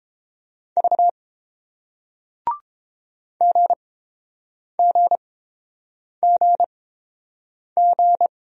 Morse code, heard as a keyed tone, about 30 wpm, spaced as for 7 wpm, 700 Hz. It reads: V GGGG